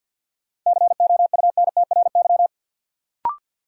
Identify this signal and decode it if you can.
Morse code, heard as a keyed tone, about 53 wpm, 700 Hz. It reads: COUNTRY
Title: Sentences from Top 300 Words - New Word Drill - 50wpm